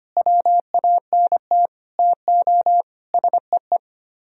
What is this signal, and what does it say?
Morse code, heard as a keyed tone, about 31 wpm, 700 Hz. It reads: WANT TO SEE
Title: Speed-Racing - Sets of 5 Words 25wpm